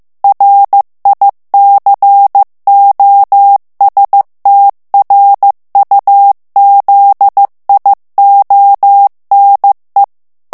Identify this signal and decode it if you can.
Morse code, heard as a keyed tone, about 15 wpm, 800 Hz. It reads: RICOSTRUZIONE
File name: ITALIEN texte clair 1 vitesse 14 mots minute